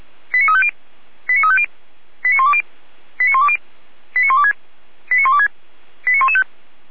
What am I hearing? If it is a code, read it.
Five-tone selective call (ZVEI), heard as quick runs of five tones, tones of about 70 ms. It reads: 89350 89350 89120 89120 89126 89126 89115
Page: SDR/ZVEI